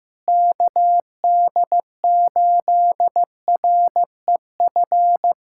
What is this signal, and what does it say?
Morse code, heard as a keyed tone, about 15 wpm, 700 Hz. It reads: KD8REF
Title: USA Callsign DE Callsign 15wpm